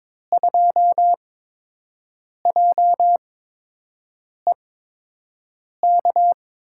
Morse code, heard as a keyed tone, about 23 wpm, spaced as for 6 wpm, 700 Hz. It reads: 2JEK
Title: USA Call Signs - 5 Characters - 8x Character Spacing - 22wpm